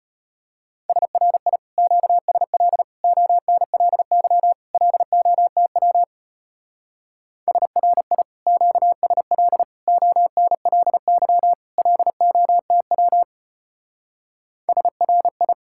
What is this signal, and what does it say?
Morse code, heard as a keyed tone, about 32 wpm, 700 Hz. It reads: SRI QSL ONLY LOTW SRI QSL ONLY LOTW SRI